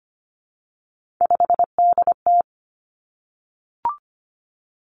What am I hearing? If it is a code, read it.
Morse code, heard as a keyed tone, about 25 wpm, 700 Hz. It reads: 5DT
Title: Three Challenging Letter-Number Combinations 25wpm